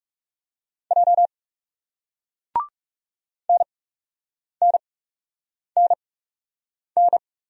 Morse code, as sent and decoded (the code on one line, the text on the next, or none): .--- / -. -. -. -.
J NNNN